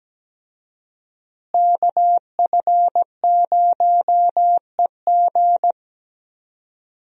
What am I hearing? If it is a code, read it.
Morse code, heard as a keyed tone, about 17 wpm, 700 Hz. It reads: KF0EG
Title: USA Call Signs - 5 Characters - Repeated 2x - 17wpm